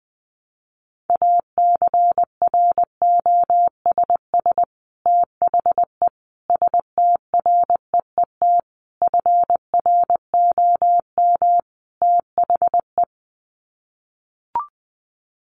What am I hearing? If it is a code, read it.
Morse code, heard as a keyed tone, about 19 wpm, 700 Hz. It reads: ACROSS THE STREET FROM THE
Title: Sets of 5 Words Encore 20wpm